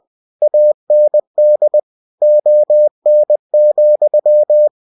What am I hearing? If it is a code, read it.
Morse code, heard as a keyed tone, about 20 wpm, 600 Hz. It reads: AND ON,
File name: winnie-20wpm